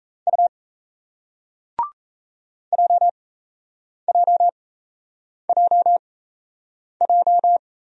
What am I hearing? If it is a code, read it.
Morse code, heard as a keyed tone, about 38 wpm, spaced as for 8 wpm, 700 Hz. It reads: U JJJJ